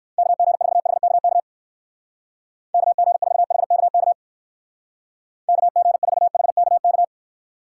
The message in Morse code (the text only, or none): KG4HKK KG4HKK KG4HKK